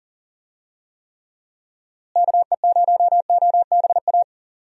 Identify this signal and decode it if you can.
Morse code, heard as a keyed tone, about 37 wpm, 700 Hz. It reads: KE0OBA
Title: USA Call Signs - 6 Characters - Repeated 2x - 40wpm